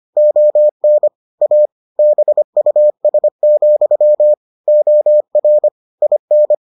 Morse code, heard as a keyed tone, about 25 wpm, 600 Hz. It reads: ON A BUS, OR IN